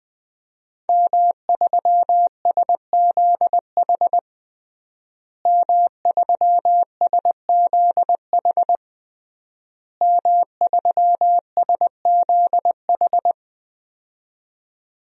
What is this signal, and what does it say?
Morse code, heard as a keyed tone, about 20 wpm, 700 Hz. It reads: M3SZH M3SZH M3SZH
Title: DXCC-100 Call Signs - 5 Characters - Repeated 3x - 20wpm